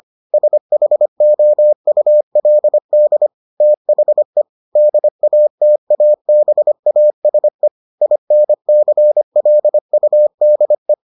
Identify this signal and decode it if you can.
Morse code, heard as a keyed tone, about 25 wpm, 600 Hz. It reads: SHOULD THE DATABASE INCLUDE